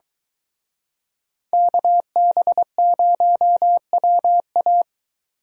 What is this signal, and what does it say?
Morse code, heard as a keyed tone, about 22 wpm, 700 Hz. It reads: KB0WA